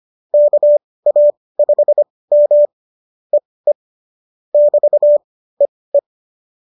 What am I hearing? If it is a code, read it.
Morse code, heard as a keyed tone, about 25 wpm, spaced as for 18 wpm, 590 Hz. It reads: KA5M EE <BT> EE